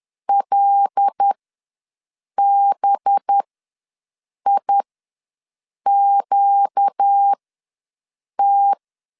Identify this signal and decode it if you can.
Morse code, heard as a keyed tone, about 10 wpm, spaced as for 6 wpm, 800 Hz. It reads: LBIQT
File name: LEZIONI / 5 LBVC / 5 LB ws 10wpm 3616